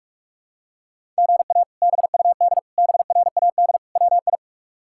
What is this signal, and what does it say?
Morse code, heard as a keyed tone, about 46 wpm, 700 Hz. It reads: GA BUD BRAD WI
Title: K1USN SST QSO Practice - 45wpm